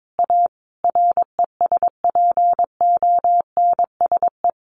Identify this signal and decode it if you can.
Morse code, heard as a keyed tone, about 22 wpm, 700 Hz. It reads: A RESPONSE
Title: Sentences from Top 1400 Words 22wpm